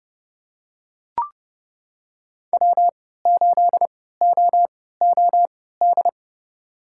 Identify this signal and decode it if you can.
Morse code, heard as a keyed tone, about 32 wpm, spaced as for 17 wpm, 700 Hz. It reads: W8OOD
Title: USA Call Signs - 5 Characters - 3x Character Spacing - 30wpm